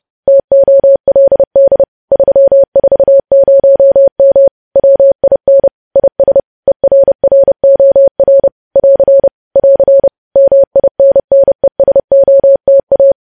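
Morse code, heard as a keyed tone, about 30 wpm, 550 Hz. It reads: TOLD 340M WIN IS ERROR <AR> <AR> MINNESOTA
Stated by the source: News headlines sent as Morse practice at 30.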